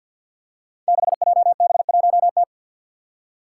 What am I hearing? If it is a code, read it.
Morse code, heard as a keyed tone, about 48 wpm, 700 Hz. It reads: 6JB1T
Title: Five Challenging Letter-Number Combinations 50wpm